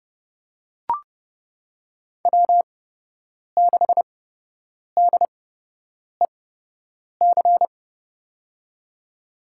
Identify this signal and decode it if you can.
Morse code, heard as a keyed tone, about 30 wpm, spaced as for 8 wpm, 700 Hz. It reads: W6DEC
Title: USA Call Signs - 5 Characters - 8x Character Spacing - 30wpm